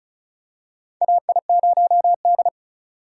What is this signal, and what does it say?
Morse code, heard as a keyed tone, about 34 wpm, 700 Hz. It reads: AI0D
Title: USA Call Signs - 4 Characters - 2x Character Spacing - 35wpm